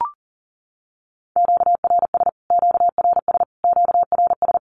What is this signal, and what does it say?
Morse code, heard as a keyed tone, about 40 wpm, 700 Hz. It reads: QRS QRS QRS
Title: Q-Codes Repeated 3x 40wpm